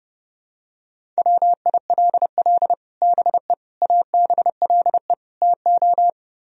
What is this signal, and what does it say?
Morse code, heard as a keyed tone, about 30 wpm, 700 Hz. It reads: WILL BE ABLE TO